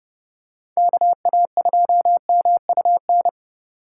Morse code, heard as a keyed tone, about 30 wpm, 700 Hz. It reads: KA2MUN